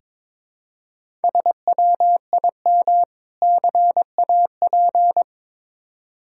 Morse code, heard as a keyed tone, about 23 wpm, 700 Hz. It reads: SWIM CAP
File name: practice / 22wpm / Difficult Words with Hints - 22wpm